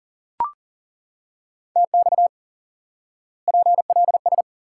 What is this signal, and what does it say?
Morse code, heard as a keyed tone, about 40 wpm, 700 Hz. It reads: TX PLS
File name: IQE Course 3 - Lesson 89 - YAGI - Two Elements - 40wpm